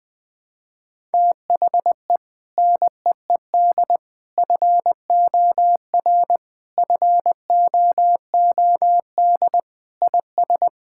Morse code, heard as a keyed tone, about 20 wpm, 700 Hz. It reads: THE NEED FOR FOOD IS
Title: Sentences with 3-Word Phrases - 20wpm